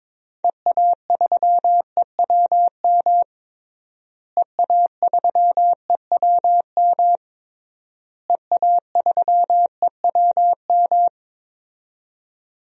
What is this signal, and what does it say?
Morse code, heard as a keyed tone, about 22 wpm, 700 Hz. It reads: EA3EWM EA3EWM EA3EWM
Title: DXCC-100 Call Signs - 6 Characters - Repeated 3x - 22wpm